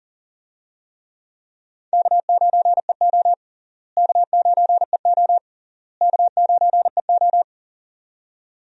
Morse code, heard as a keyed tone, about 37 wpm, 700 Hz. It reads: K9EO K9EO K9EO
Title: USA Call Signs - 4 Characters - Repeated 3x - 40wpm